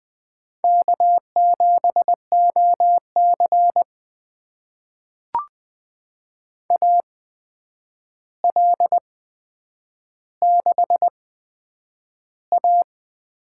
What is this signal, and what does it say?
Morse code, heard as a keyed tone, about 20 wpm, spaced as for 9 wpm, 700 Hz. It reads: K7OC AL6A
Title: USA Call Signs - 4 Characters - 8x Character Spacing - 20wpm